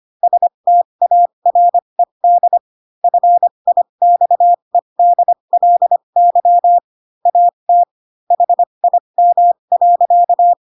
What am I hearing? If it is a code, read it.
Morse code, heard as a keyed tone, about 25 wpm, spaced as for 22 wpm, 700 Hz. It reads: STARED FIXEDLY AT HIM.
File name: Pinocchio / Pinocchio ch3 18 25 ik1hge